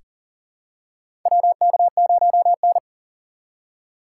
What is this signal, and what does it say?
Morse code, heard as a keyed tone, about 39 wpm, 700 Hz. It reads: WK0N
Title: USA Call Signs - 4 Characters - 4x Character Spacing - 40wpm